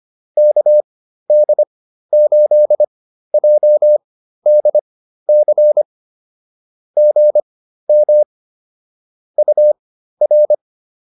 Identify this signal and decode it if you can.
Morse code, heard as a keyed tone, about 25 wpm, spaced as for 13 wpm, 590 Hz. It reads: KD8JDC GM UR